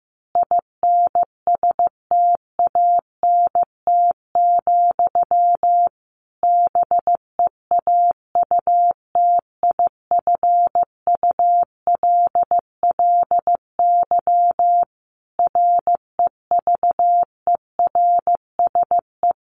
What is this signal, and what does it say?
Morse code, heard as a keyed tone, about 15 wpm, 700 Hz. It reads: INSTANT, BEAUTIFULLY REVERSE
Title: A Sound Of Thunder 15wpm